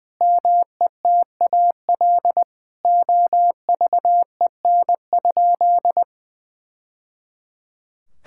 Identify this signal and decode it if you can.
Morse code, heard as a keyed tone, about 20 wpm, 700 Hz. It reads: METAL OVEN?
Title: Sentences from Top 1000 Words - Review Part 6 - 20wpm